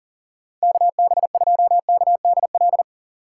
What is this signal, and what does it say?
Morse code, heard as a keyed tone, about 40 wpm, 700 Hz. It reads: KB2KDL
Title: USA Call Signs - 6x Character Spacing - 40wpm